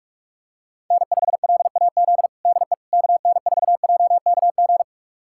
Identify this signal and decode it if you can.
Morse code, heard as a keyed tone, about 46 wpm, 700 Hz. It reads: N5LAZ DE KN4JKG
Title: USA Callsign DE Callsign 45wpm